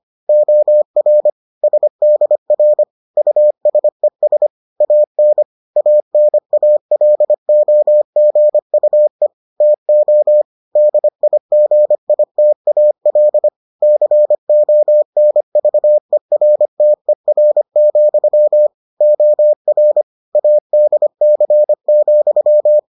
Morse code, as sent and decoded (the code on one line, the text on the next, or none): --- .-. / ... -.. .-. / ..- ... ... / .- -. / .- -. .- .-.. --- --. ..- . / - --- / -.. .. --. .. - .- .-.. / -.-. --- -. ...- . .-. - . .-. --..-- / --- .-. / .- -.. -.-. --..--
OR SDR USES AN ANALOGUE TO DIGITAL CONVERTER, OR ADC,